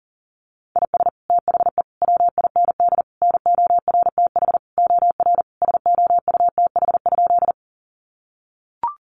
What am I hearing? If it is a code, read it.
Morse code, heard as a keyed tone, about 40 wpm, 700 Hz. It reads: IS THE WIND NORTH OR SOUTH?
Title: Sentences from Top 400 Words 40wpm